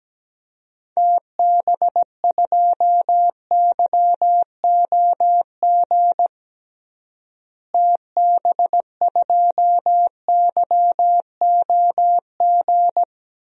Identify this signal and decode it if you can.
Morse code, heard as a keyed tone, about 17 wpm, 700 Hz. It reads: TB2YOG TB2YOG
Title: DXCC-100 Call Signs - 6 Characters - Repeated 2x - 17wpm